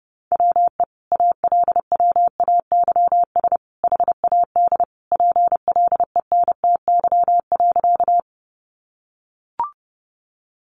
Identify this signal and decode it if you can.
Morse code, heard as a keyed tone, about 30 wpm, 700 Hz. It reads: WE ALWAYS HAD PLENTY.
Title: Sentences from Top 1300 Words 30wpm